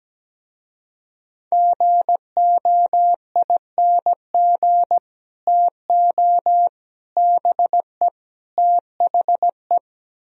Morse code, heard as a keyed tone, about 17 wpm, 700 Hz. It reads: GOING TO BE THE
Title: Speed-Racing - Sets of 4 Words 17wpm